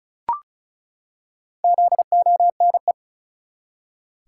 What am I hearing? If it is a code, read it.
Morse code, heard as a keyed tone, about 34 wpm, 700 Hz. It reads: ZONE
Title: Four Letter Words - 35wpm